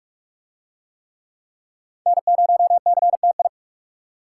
Morse code, heard as a keyed tone, about 44 wpm, 700 Hz. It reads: N0CTI